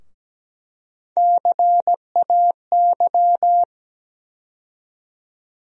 Morse code, heard as a keyed tone, about 17 wpm, 700 Hz. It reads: CAY
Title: USA Call Sign Postfixes - 3 Letters 17wpm